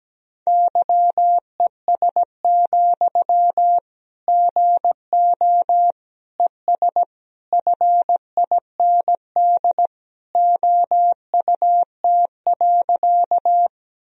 Morse code, heard as a keyed tone, about 17 wpm, 700 Hz. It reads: YES, GO ES FIND OUT.
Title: Sentences from Top 500 Words and Abbreviations 17wpm